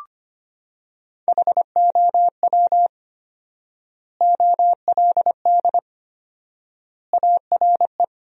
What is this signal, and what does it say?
Morse code, heard as a keyed tone, about 25 wpm, 700 Hz. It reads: HOW OLD ARE
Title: 4x Word-Spacing - Sentences from Top 200 Words - 25wpm